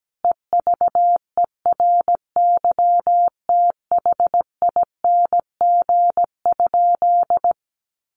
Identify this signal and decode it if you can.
Morse code, heard as a keyed tone, about 17 wpm, 700 Hz. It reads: EVERYTHING?
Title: Sentences from Top 700 Words 17wpm